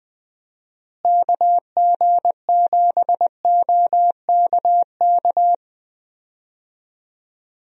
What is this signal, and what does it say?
Morse code, heard as a keyed tone, about 20 wpm, 700 Hz. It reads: KG7OKK
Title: USA Call Signs - 6 Characters - Repeated 2x - 20wpm